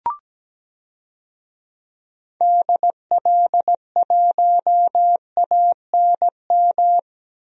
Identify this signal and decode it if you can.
Morse code, heard as a keyed tone, about 17 wpm, 700 Hz. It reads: DL1ANM
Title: DXCC-100 Call Signs - 6 Characters - 17wpm